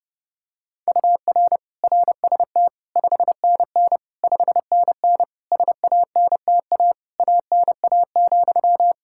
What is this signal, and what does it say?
Morse code, heard as a keyed tone, about 30 wpm, 700 Hz. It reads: UR RST 5NN 5NN SANTA ANA,
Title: Practice QSO - Exchange 1 - Beginning - 30wpm